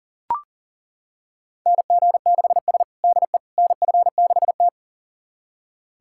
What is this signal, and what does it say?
Morse code, heard as a keyed tone, about 40 wpm, 700 Hz. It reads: NG6S DE NF6T
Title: USA Callsign DE Callsign - 4 Characters - 40wpm